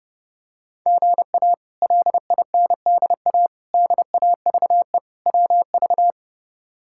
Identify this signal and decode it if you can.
Morse code, heard as a keyed tone, about 32 wpm, 700 Hz. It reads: GA LINDA DAVE WV
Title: K1USN SST QSO Practice - 30wpm